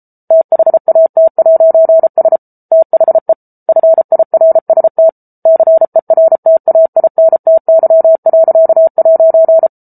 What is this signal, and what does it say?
Morse code, heard as a keyed tone, about 34 wpm, 650 Hz. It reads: THAT'S THE FIRST CERTAINTY.'